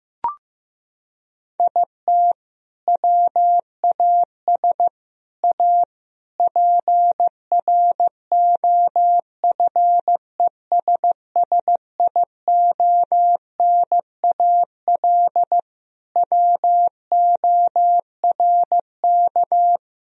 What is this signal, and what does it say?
Morse code, heard as a keyed tone, about 16 wpm, 700 Hz. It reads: IT WAS A PROFESSIONAL WORK
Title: Sentences from Top 1000 Words - Review Part 4 - 15wpm